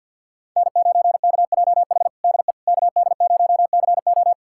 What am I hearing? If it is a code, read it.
Morse code, heard as a keyed tone, about 46 wpm, 700 Hz. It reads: N9KJH DE KD0CO
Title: USA Callsign DE Callsign - 5 Characters - 50wpm